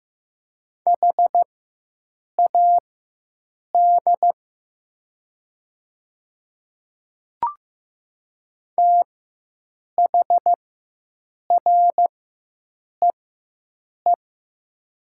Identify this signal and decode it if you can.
Morse code, heard as a keyed tone, about 15 wpm, spaced as for 7 wpm, 700 Hz. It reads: HAD THREE
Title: 4x Character Spacing - Top 200 Words - 15wpm